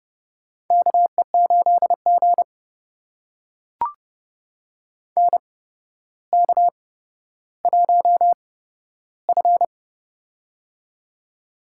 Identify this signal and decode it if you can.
Morse code, heard as a keyed tone, about 29 wpm, spaced as for 13 wpm, 700 Hz. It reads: KE8G NK1F
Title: USA Call Signs - 4 Characters - 8x Character Spacing - 30wpm